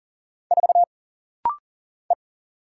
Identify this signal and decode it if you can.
Morse code, heard as a keyed tone, about 40 wpm, 700 Hz. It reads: 4 E
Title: Single Letter-Number Alt - Rapid-Fire 40wpm